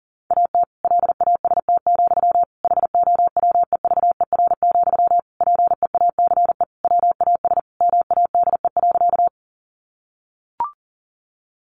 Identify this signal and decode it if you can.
Morse code, heard as a keyed tone, about 40 wpm, 700 Hz. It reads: AT LAST, HOWEVER, PEACE WAS MADE.
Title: Sentences from Top 1500 Words 40wpm